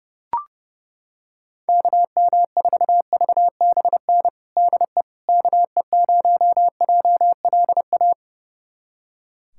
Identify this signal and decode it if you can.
Morse code, heard as a keyed tone, about 30 wpm, 700 Hz. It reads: KM4VBN DE KE0JLA
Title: USA Callsign DE Callsign - 6 Characters - 30wpm